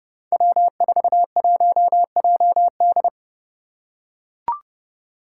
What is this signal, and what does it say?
Morse code, heard as a keyed tone, about 30 wpm, 700 Hz. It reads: W41JD
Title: Five Challenging Letter-Number Combinations 30wpm